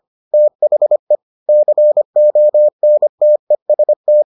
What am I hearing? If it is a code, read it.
Morse code, heard as a keyed tone, about 25 wpm, 600 Hz. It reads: THE CONTEST